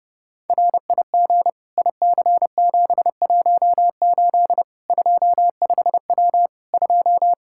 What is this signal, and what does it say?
Morse code, heard as a keyed tone, about 30 wpm, 700 Hz. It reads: RIG IC718 25W 2
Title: Practice QSO - Exchange 2 - Info - 30wpm